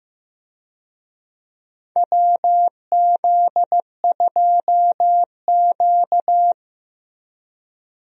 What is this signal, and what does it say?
Morse code, heard as a keyed tone, about 15 wpm, 700 Hz. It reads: WZ2Q
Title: USA Call Signs - 4 Characters - Repeated 2x - 15wpm